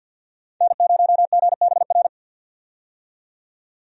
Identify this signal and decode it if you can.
Morse code, heard as a keyed tone, about 46 wpm, 700 Hz. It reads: N0GBR
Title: USA Call Signs - 5 Characters - 50wpm